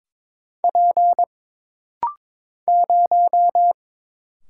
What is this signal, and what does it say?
Morse code, heard as a keyed tone, about 23 wpm, 700 Hz. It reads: P 0
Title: Single Letter-Number Alt - Rapid-Fire 22wpm